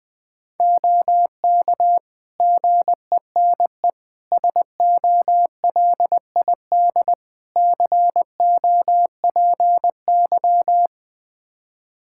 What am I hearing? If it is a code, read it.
Morse code, heard as a keyed tone, about 21 wpm, 700 Hz. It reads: OK GENE SOLID COPY
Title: Practice QSO - Exchange 3 - More Info - 20wpm